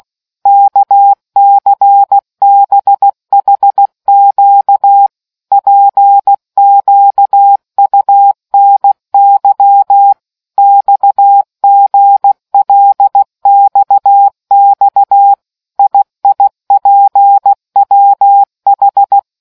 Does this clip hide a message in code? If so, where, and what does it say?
Yes: Morse code from 0.5 to 19.2 s, heard as a keyed tone, about 16 wpm, 800 Hz. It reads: KCBHQ PQUNY XGLXX IIPWH